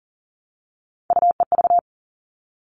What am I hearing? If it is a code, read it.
Morse code, heard as a keyed tone, about 40 wpm, 700 Hz. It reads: UEV
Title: All Cut Numbers - 3-Digit 40wpm